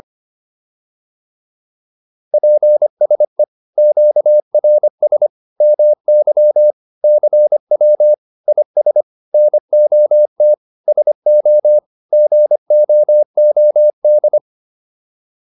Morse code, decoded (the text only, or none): PSE QRS MY CW IS NOT SO GOOD